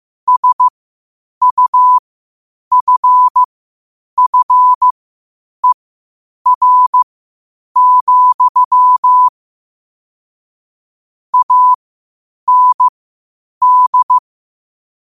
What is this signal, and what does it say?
Morse code, heard as a keyed tone, about 15 wpm, spaced as for 8 wpm, 1000 Hz. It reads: SUFFER, AND
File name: qotdfiles / 8wpm